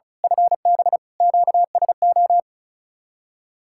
Morse code, heard as a keyed tone, about 33 wpm, 700 Hz. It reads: FB QSO
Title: Speed-Racing Common QSO Phrases - 35wpm